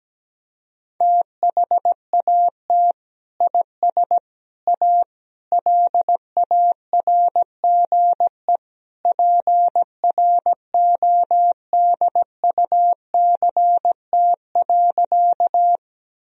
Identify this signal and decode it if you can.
Morse code, heard as a keyed tone, about 17 wpm, 700 Hz. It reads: THAT IS A LARGE PRODUCT.
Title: Sentences from Top 1000 Words - Review Part 2 - 17wpm